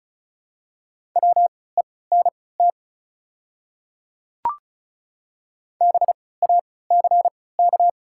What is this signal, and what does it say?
Morse code, heard as a keyed tone, about 36 wpm, spaced as for 20 wpm, 700 Hz. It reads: WENT BACK
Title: 3x Character Spacing - Top 200 Words - 35wpm